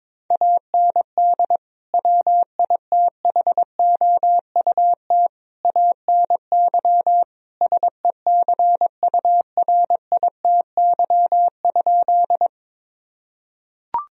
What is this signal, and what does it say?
Morse code, heard as a keyed tone, about 22 wpm, 700 Hz. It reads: AND WITHOUT ANY SECURITY?
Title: Sentences from Top 1000 Words - Review Part 4 - 22wpm